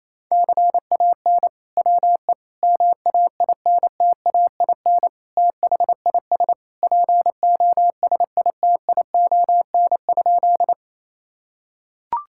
Morse code, heard as a keyed tone, about 28 wpm, 700 Hz. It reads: CAN WE MAINTAIN THIS POSITION?
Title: Sentences from Top 1000 Words - Review Part 5 - 28wpm